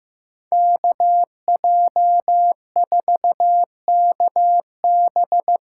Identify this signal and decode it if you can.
Morse code, heard as a keyed tone, about 15 wpm, 700 Hz. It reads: KJ4KB
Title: USA Call Signs - 6 Characters - 8x Character Spacing - 15wpm